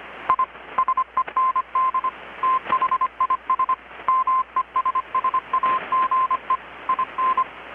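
Morse code, heard as a keyed tone, about 25 wpm, 1040 Hz. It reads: I SEND THIS MESSAGE IN
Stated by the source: Morse from KPH's Night of Nights XX maritime radiotelegraphy event, received in Chiapas, Mexico about 3,800 km away on a Kenwood R-600 with a grounded 10 m long-wire antenna.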